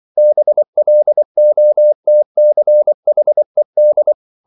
Morse code, heard as a keyed tone, about 24 wpm, 600 Hz. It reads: BLOTCHED